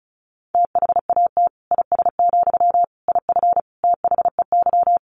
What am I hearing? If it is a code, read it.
Morse code, heard as a keyed tone, about 34 wpm, 700 Hz. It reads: THAT IS, IF THEY
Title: Sentences from Top 1100 Words 35wpm